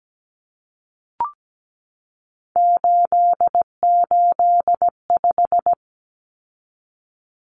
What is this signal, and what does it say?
Morse code, heard as a keyed tone, about 17 wpm, 700 Hz. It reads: 885
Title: Numbers - 3-Digit 17wpm